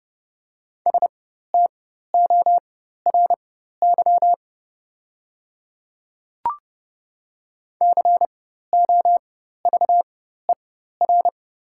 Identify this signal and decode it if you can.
Morse code, heard as a keyed tone, about 30 wpm, spaced as for 14 wpm, 700 Hz. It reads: STORY COVER
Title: 4x Character Spacing - Top 300 Words - 30wpm